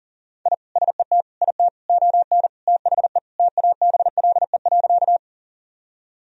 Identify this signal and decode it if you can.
Morse code, heard as a keyed tone, about 40 wpm, 700 Hz. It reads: I SET IT ON THE TABLE.
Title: Sentences from Top 1000 Words - Review Part 2 - 40wpm